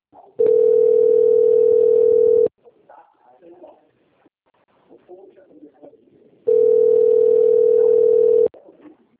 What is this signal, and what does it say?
Ringback tone (North America), 440 + 480 Hz, with speech.
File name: out- - - - - .5017